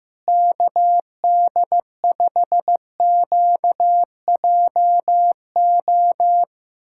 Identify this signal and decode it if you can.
Morse code, heard as a keyed tone, about 16 wpm, 700 Hz. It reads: KD5QJO
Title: USA Callsign DE Callsign 15wpm